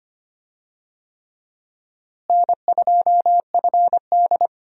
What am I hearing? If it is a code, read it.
Morse code, heard as a keyed tone, about 25 wpm, 700 Hz. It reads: N2FD
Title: USA Call Signs - 4 Characters - Repeated 3x - 25wpm